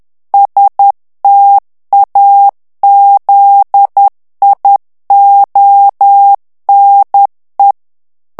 Morse code, heard as a keyed tone, about 10 wpm, 800 Hz. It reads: STAZIONE
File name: morse mp3 / ITALIEN texte clair 1 vitesse 10 mots minute